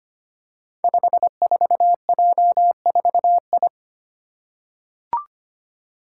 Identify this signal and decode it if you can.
Morse code, heard as a keyed tone, about 25 wpm, 700 Hz. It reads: 54J4I
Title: Five Challenging Letter-Number Combinations 25wpm